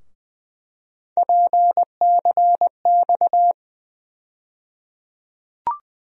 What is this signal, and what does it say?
Morse code, heard as a keyed tone, about 20 wpm, 700 Hz. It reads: PCX
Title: USA Call Sign Postfixes - 3 Letters 20wpm